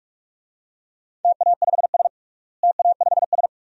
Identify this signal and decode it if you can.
Morse code, heard as a keyed tone, about 46 wpm, 700 Hz. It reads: TA5S TA5S
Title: DXCC-100 Call Signs - 4 Characters - Repeated 2x - 45wpm